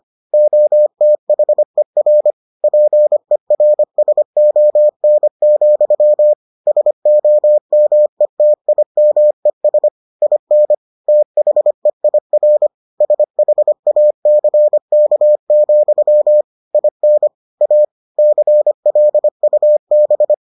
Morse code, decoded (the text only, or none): OTHER PERSON, SOMETIMES IN THEIR SHACK, IN A CLUB